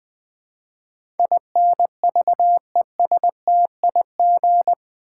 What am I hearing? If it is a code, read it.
Morse code, heard as a keyed tone, about 20 wpm, 700 Hz. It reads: INVESTIG
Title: Build A Word - 4 Letters At A Time - 12-Letter Words - 20wpm